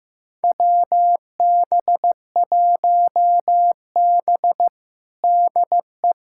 Morse code, heard as a keyed tone, about 15 wpm, 700 Hz. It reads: WB1B DE
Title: USA Callsign DE Callsign - 4 Characters - 15wpm